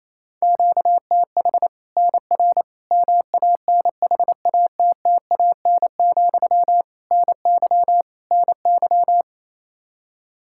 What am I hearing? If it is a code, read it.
Morse code, heard as a keyed tone, about 28 wpm, 700 Hz. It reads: QTH NR MANHATTAN, NY NY
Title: Practice QSO - Exchange 1 - Beginning - 28wpm